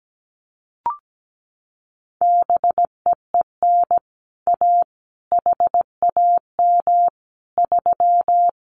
Morse code, heard as a keyed tone, about 17 wpm, 700 Hz. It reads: BEEN A HAM 3
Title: Common QSO Phrases Encore 17wpm